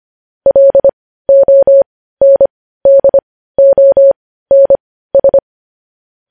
Morse code, heard as a keyed tone, about 25 wpm, spaced as for 15 wpm, 550 Hz. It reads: LONDONS
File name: podcast / News headlines-1x-25x15